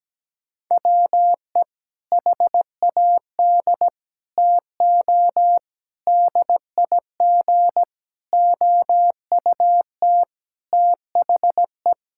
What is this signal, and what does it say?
Morse code, heard as a keyed tone, about 17 wpm, 700 Hz. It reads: WE HAD TO DIG OUT THE
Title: Sentences from Top 1600 Words 17wpm